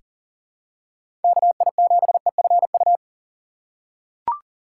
Morse code, heard as a keyed tone, about 37 wpm, 700 Hz. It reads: KI7EFU
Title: USA Call Signs - 2x Character Spacing - 40wpm